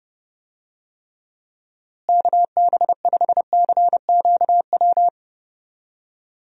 Morse code, heard as a keyed tone, about 30 wpm, 700 Hz. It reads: KB5CQW
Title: USA Call Signs - 6 Characters - Repeated 2x - 30wpm